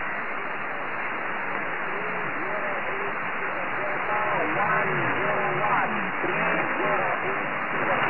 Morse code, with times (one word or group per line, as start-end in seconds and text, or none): none